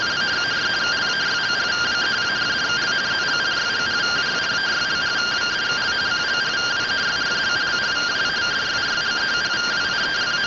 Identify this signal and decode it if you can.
Two-tone teleprinter signal, about 170 Hz shift.